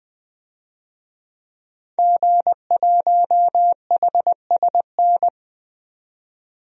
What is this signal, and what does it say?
Morse code, heard as a keyed tone, about 20 wpm, 700 Hz. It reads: G1HSN
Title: DXCC-100 Call Signs - 5 Characters - 20wpm